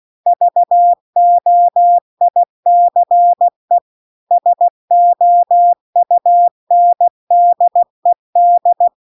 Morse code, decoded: VOICE SOUNDED

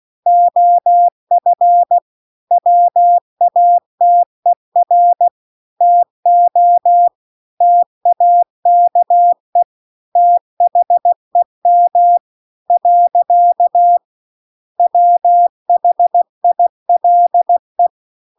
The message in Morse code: OF WATER TO TAKE THEM . WHILE

